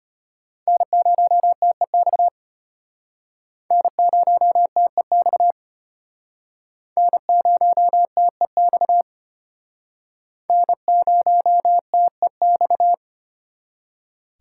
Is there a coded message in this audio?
Yes: Morse code from 0.7 to 12.9 s, heard as a keyed tone, about 34 wpm, 700 Hz. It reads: N0TEX N0TEX N0TEX N0TEX